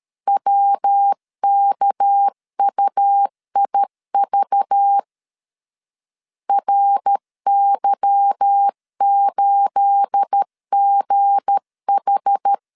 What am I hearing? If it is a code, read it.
Morse code, heard as a keyed tone, about 13 wpm, 800 Hz. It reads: WKUIV RY8GH
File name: LEZIONI / 9 28 12wpm 2351